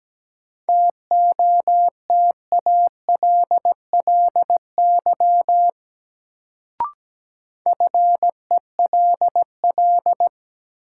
Morse code, heard as a keyed tone, about 17 wpm, 700 Hz. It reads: TOTALLY FELL
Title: Sentences from Top 1600 Words - New Word Drill - 17wpm